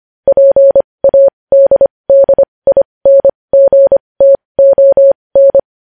Morse code, heard as a keyed tone, about 25 wpm, spaced as for 20 wpm, 550 Hz. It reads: PADDINGTON
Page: news headlines in morse code at 20 wpm